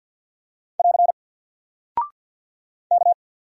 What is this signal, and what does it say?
Morse code, heard as a keyed tone, about 50 wpm, 700 Hz. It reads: <AR> K
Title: Single Character (Letter:Number:Punctuation:Prosign) 50wpm